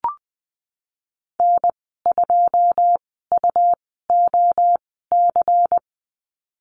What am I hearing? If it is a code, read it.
Morse code, heard as a keyed tone, about 20 wpm, spaced as for 14 wpm, 700 Hz. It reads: N2UOC
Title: 140 Lesson 23g - Letter f - Call Signs 20wpm